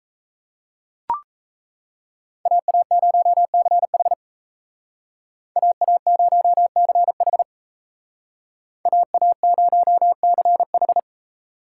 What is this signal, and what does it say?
Morse code, heard as a keyed tone, about 35 wpm, 700 Hz. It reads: AA0CH AA0CH AA0CH
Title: Speed Racing - USA 2x2 Call Signs 28wpm